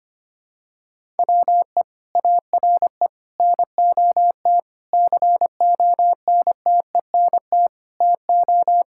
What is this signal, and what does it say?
Morse code, heard as a keyed tone, about 25 wpm, 700 Hz. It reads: WE ARE NOT CONTENT TO
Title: Sentences from Top 1000 Words - Review Part 6 - 25wpm